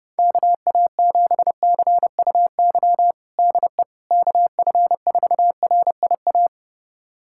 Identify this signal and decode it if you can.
Morse code, heard as a keyed tone, about 32 wpm, 700 Hz. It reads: KA7CUY DE KF4RIA
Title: USA Callsign DE Callsign 30wpm